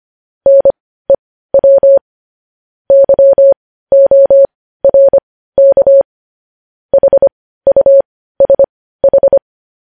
Morse code, heard as a keyed tone, about 25 wpm, spaced as for 15 wpm, 550 Hz. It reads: NEW YORK HUSH
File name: podcast / News headlines-1x-25x15